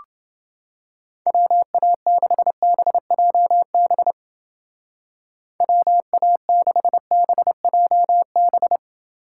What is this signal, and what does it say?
Morse code, heard as a keyed tone, about 30 wpm, 700 Hz. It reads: WA6BJB WA6BJB